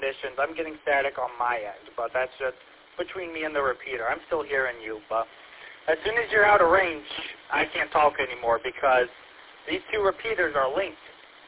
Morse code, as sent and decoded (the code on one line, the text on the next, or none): none